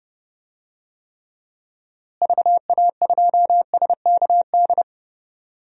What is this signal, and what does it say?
Morse code, heard as a keyed tone, about 32 wpm, 700 Hz. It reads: VA2SKD
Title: DXCC-100 Call Signs - 6 Characters - Repeated 2x - 30wpm